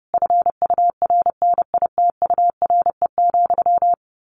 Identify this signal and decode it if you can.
Morse code, heard as a keyed tone, about 30 wpm, 700 Hz. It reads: FURNITURE,
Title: A Sound Of Thunder 30wpm